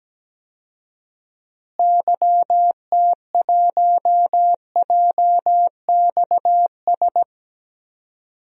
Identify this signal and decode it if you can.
Morse code, heard as a keyed tone, about 17 wpm, 700 Hz. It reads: YT1JXS
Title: DXCC-100 Call Signs - 6 Characters - Repeated 2x - 17wpm